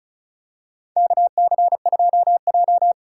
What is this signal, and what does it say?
Morse code, heard as a keyed tone, about 34 wpm, 700 Hz. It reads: KC2J
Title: USA Call Signs - 4 Characters - 2x Character Spacing - 35wpm